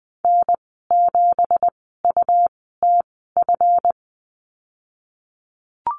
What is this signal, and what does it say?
Morse code, heard as a keyed tone, about 19 wpm, spaced as for 14 wpm, 700 Hz. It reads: N7UTF
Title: 179 Lesson 29g - Slash - Call Signs 20wpm